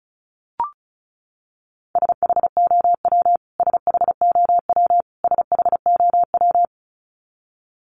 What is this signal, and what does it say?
Morse code, heard as a keyed tone, about 34 wpm, 700 Hz. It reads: SHOW SHOW SHOW
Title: Top 500 Words Repeated 35wpm